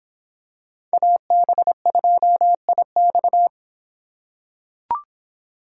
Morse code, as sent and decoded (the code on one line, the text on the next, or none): .- -... ..--- .. -..-
AB2IX